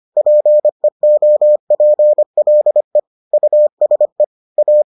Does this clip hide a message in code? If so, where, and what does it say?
Yes: Morse code from 0.2 to 4.8 s, heard as a keyed tone, about 25 wpm, 600 Hz. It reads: PEOPLE USE A